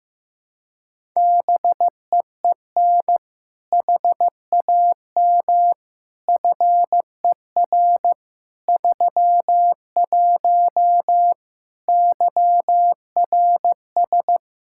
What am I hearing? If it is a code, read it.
Morse code, heard as a keyed tone, about 15 wpm, 700 Hz. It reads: BEEN HAM FER 31 YRS
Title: Practice QSO - Exchange 3 - More Info - 15wpm